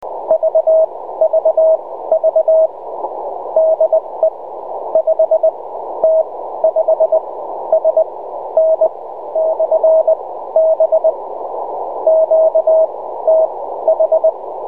Morse code, heard as a keyed tone, about 20 wpm, spaced as for 13 wpm, 640 Hz. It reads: VVV DE 5T5SN/B QTH
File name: Enregistrement QSO / 5t5sn b-2007